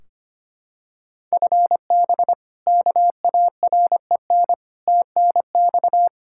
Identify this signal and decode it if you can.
Morse code, heard as a keyed tone, about 25 wpm, 700 Hz. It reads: FB KAREN TNX
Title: K1USN SST QSO Practice 25wpm